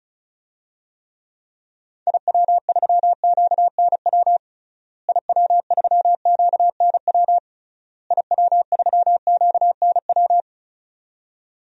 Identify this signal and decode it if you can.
Morse code, heard as a keyed tone, about 34 wpm, 700 Hz. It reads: IW3QNW IW3QNW IW3QNW
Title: DXCC-100 Call Signs - 6 Characters - Repeated 3x - 35wpm